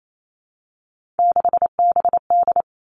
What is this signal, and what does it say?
Morse code, heard as a keyed tone, about 28 wpm, 690 Hz. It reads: 6BD
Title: Three Challenging Letter-Number Combinations 28wpm